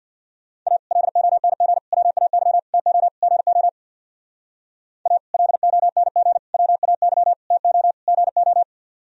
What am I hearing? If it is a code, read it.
Morse code, heard as a keyed tone, about 63 wpm, 700 Hz. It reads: A LONG WAY TO GO A LONG WAY TO GO